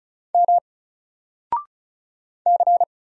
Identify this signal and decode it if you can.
Morse code, heard as a keyed tone, about 34 wpm, 700 Hz. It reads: M C
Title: Single Letter-Number Alt 35wpm